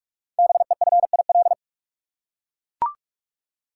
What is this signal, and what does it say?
Morse code, heard as a keyed tone, about 44 wpm, 700 Hz. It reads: BEFIL
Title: Code Groups - 5 Letters - 45wpm